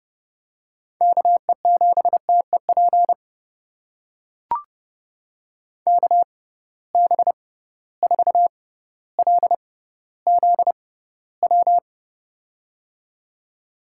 Morse code, heard as a keyed tone, about 30 wpm, spaced as for 15 wpm, 700 Hz. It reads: KE7TEP KB4LZW
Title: USA Call Signs - 6 Characters - 6x Character Spacing - 30wpm